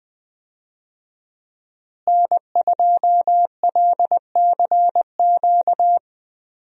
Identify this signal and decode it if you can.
Morse code, heard as a keyed tone, about 20 wpm, 700 Hz. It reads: N2LCQ
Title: USA Call Signs - 5 Characters - Repeated 2x - 20wpm